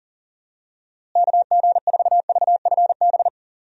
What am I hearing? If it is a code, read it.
Morse code, heard as a keyed tone, about 40 wpm, 700 Hz. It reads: KG4VFB
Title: USA Call Signs - 6 Characters - Repeated 2x - 40wpm